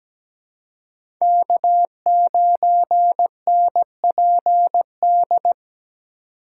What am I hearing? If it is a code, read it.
Morse code, heard as a keyed tone, about 17 wpm, 700 Hz. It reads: K9NPD
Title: USA Call Signs - 5 Characters - 4x Character Spacing - 17wpm